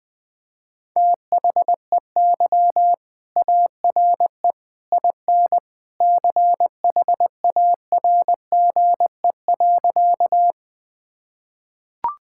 Morse code, heard as a keyed tone, about 21 wpm, 700 Hz. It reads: THEY ARE IN CHARGE.
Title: Sentences from Top 1000 Words - Review Part 6 - 20wpm